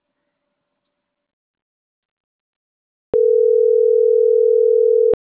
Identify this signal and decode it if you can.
Ringback tone (North America), 440 + 480 Hz.
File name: CENTRAL MAYAB / out vanessa.loyola